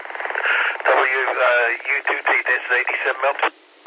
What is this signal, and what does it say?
Emergency Alert System (SAME) tones, heard as harsh buzzing bursts, with speech. It is heard from 0.3 to 3.1 s.